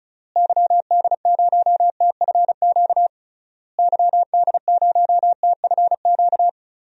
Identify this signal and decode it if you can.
Morse code, heard as a keyed tone, about 36 wpm, 700 Hz. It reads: YD0TFQ YD0TFQ
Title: DXCC-100 Call Signs - 6 Characters - Repeated 3x - 35wpm